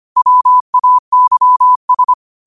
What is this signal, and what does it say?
Morse code, heard as a keyed tone, about 25 wpm, 1000 Hz. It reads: WAYS
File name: qotdfiles / 25wpm